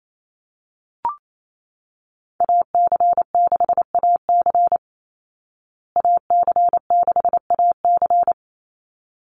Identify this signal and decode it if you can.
Morse code, heard as a keyed tone, about 28 wpm, 700 Hz. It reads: AC6AC AC6AC
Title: Call Signs Repeated 3x - 28wpm